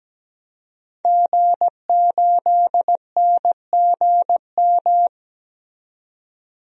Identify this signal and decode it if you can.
Morse code, heard as a keyed tone, about 17 wpm, 700 Hz. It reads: G8NGM
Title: DXCC-100 Call Signs - 5 Characters - 17wpm